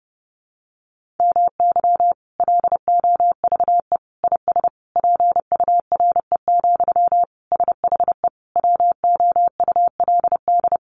Morse code, heard as a keyed tone, about 30 wpm, 700 Hz. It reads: MY LOVE IS PURE, SHE WOULD
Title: Sentences from Top 1500 Words 30wpm